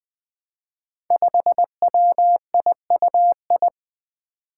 Morse code, heard as a keyed tone, about 20 wpm, 700 Hz. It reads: 5WIUI
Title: Five Challenging Letter-Number Combinations 20wpm